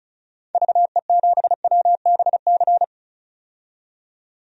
Morse code, heard as a keyed tone, about 34 wpm, 700 Hz. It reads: VE7WBC
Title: DXCC-100 Call Signs - 6 Characters - 35wpm